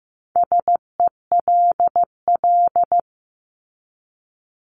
Morse code, heard as a keyed tone, about 15 wpm, 700 Hz. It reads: SELL
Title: Top 1001-1250 Words 15wpm